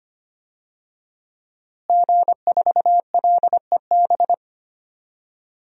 Morse code, heard as a keyed tone, about 25 wpm, 700 Hz. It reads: G4LEB
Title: DXCC-100 Call Signs - 5 Characters - Repeated 3x - 25wpm